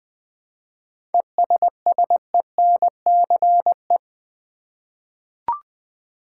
Morse code, heard as a keyed tone, about 20 wpm, 700 Hz. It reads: ESSENCE
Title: 7 Letter Words - First 3 Letters Given - 20wpm